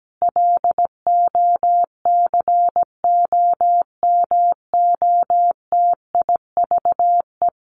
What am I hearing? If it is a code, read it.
Morse code, heard as a keyed tone, about 17 wpm, 700 Hz. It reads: LOCOMOTIVE
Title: A Sound Of Thunder 17wpm